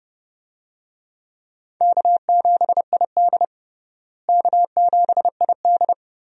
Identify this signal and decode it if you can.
Morse code, heard as a keyed tone, about 30 wpm, 700 Hz. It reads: K7ID K7ID
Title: USA Call Signs - 4 Characters - Repeated 2x - 30wpm